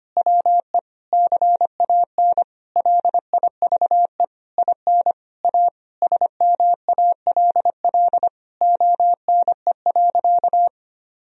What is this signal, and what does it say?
Morse code, heard as a keyed tone, about 25 wpm, 700 Hz. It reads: WE CAN LIVE IN A SMALL ONE.
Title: Sentences from Top 1000 Words - Review Part 1 - 25wpm